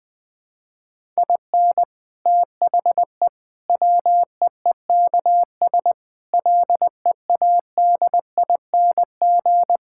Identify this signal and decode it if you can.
Morse code, heard as a keyed tone, about 20 wpm, 700 Hz. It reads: IN THE WEEKS LEADING
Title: Sentences with 7-Word Phrases - Spoken Beginning - 20wpm